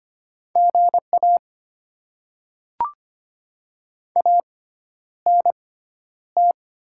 Morse code, heard as a keyed tone, about 25 wpm, spaced as for 11 wpm, 700 Hz. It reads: GA ANT
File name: Top 20 QSO Elements - 6x Character Spacing - 25wpm